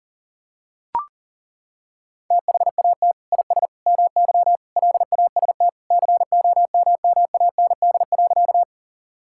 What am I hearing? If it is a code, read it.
Morse code, heard as a keyed tone, about 40 wpm, 700 Hz. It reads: THAT IS MY LAST COMMAND.
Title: Sentences from Top 1600 Words 40wpm